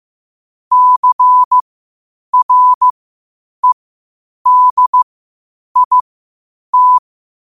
Morse code, heard as a keyed tone, about 15 wpm, spaced as for 8 wpm, 1000 Hz. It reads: CREDIT
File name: qotdfiles / 8wpm